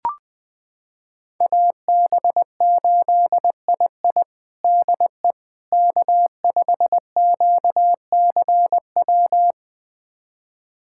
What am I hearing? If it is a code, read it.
Morse code, heard as a keyed tone, about 21 wpm, 700 Hz. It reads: AB8II DE K5QCW
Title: USA Callsign DE Callsign - 5 Characters - 20wpm